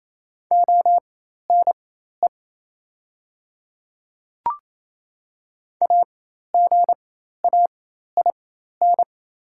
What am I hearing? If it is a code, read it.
Morse code, heard as a keyed tone, about 28 wpm, spaced as for 13 wpm, 700 Hz. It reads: ONE AGAIN
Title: 4x Character Spacing - Top 200 Words - 28wpm